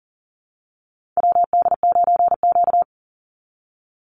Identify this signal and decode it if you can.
Morse code, heard as a keyed tone, about 38 wpm, 700 Hz. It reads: WD9Q
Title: Call Signs - 40wpm